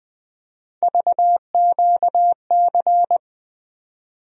Morse code, heard as a keyed tone, about 20 wpm, 700 Hz. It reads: VQC
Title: USA Call Sign Postfixes - 3 Letters 20wpm